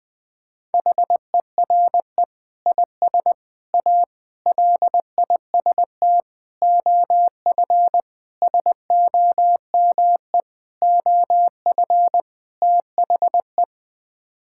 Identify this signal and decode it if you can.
Morse code, heard as a keyed tone, about 20 wpm, 700 Hz. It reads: HERE IS A LIST OF SOME OF THE
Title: Sets of 8 Words - 20wpm